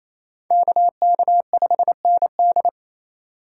Morse code, heard as a keyed tone, about 28 wpm, 700 Hz. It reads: KK5ND
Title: USA Call Signs - 5 Characters - 2x Character Spacing - 28wpm